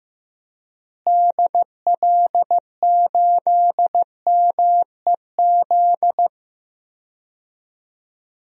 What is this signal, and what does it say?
Morse code, heard as a keyed tone, about 15 wpm, 700 Hz. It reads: DL8MEZ
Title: DXCC-100 Call Signs - 6 Characters - Repeated 2x - 15wpm